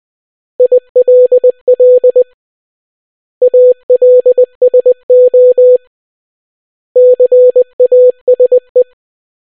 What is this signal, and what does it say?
Morse code, heard as a keyed tone, about 20 wpm, 500 Hz. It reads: ILL ALSO CASE